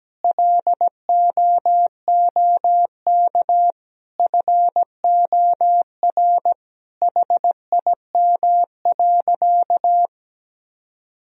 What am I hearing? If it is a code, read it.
Morse code, heard as a keyed tone, about 17 wpm, 700 Hz. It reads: LOOK FOR HIM.
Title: Sentences from Top 1000 Words - Review Part 4 - 17wpm